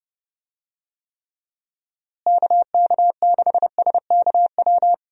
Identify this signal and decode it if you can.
Morse code, heard as a keyed tone, about 31 wpm, 700 Hz. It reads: KK6SKW
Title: USA Call Signs - 6 Characters - Repeated 3x - 30wpm